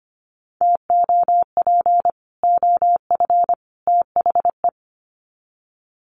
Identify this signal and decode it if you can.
Morse code, heard as a keyed tone, about 25 wpm, 700 Hz. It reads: TOP OF THE
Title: Sets of 3 Words 25wpm